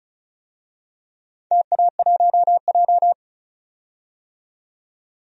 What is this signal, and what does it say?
Morse code, heard as a keyed tone, about 36 wpm, 700 Hz. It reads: TA1J
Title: DXCC-100 Call Signs - 4 Characters - 35wpm